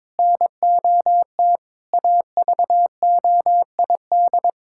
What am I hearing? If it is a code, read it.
Morse code, heard as a keyed tone, about 22 wpm, 700 Hz. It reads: NOT AVOID
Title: Sentences from Top 1000 Words - Review Part 6 - 22wpm